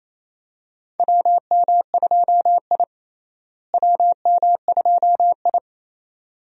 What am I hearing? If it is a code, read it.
Morse code, heard as a keyed tone, about 28 wpm, 700 Hz. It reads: WM2I WM2I
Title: USA Call Signs - 4 Characters - Repeated 2x - 28wpm